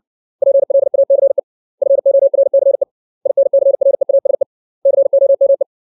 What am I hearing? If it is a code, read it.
Morse code, heard as a keyed tone, about 60 wpm, 550 Hz. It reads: PLACE FORCE INCREASE COME
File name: 632 words / 632 Words 60WPM